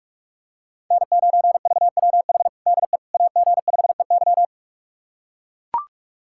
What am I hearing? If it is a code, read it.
Morse code, heard as a keyed tone, about 44 wpm, 700 Hz. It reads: N9VWH DE AG5EY
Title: USA Callsign DE Callsign - 5 Characters - 45wpm